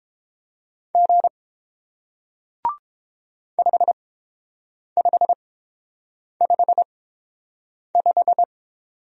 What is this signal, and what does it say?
Morse code, heard as a keyed tone, about 31 wpm, spaced as for 7 wpm, 700 Hz. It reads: G 5555